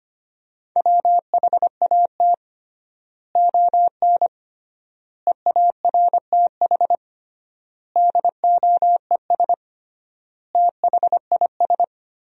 Morse code, heard as a keyed tone, about 25 wpm, 700 Hz. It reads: WHAT ON EARTH DOES THIS
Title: 3x Word-Spacing - Sentences from Top 200 Words - 25wpm